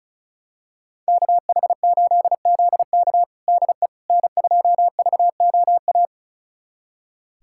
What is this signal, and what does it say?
Morse code, heard as a keyed tone, about 34 wpm, 700 Hz. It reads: KH8ZK DE N2VOA
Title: USA Callsign DE Callsign - 5 Characters - 35wpm